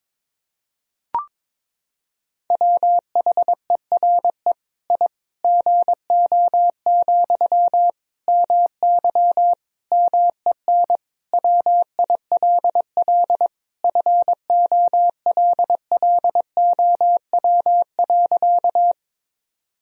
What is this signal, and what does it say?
Morse code, heard as a keyed tone, about 22 wpm, 700 Hz. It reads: WHERE I GO, MY MEN WILL FOLLOW.
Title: Sentences from Top 1000 Words - Review Part 1 - 22wpm